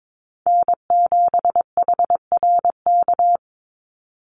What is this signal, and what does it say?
Morse code, heard as a keyed tone, about 22 wpm, 700 Hz. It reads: N7HRK
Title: Call Signs Repeated 2x - 22wpm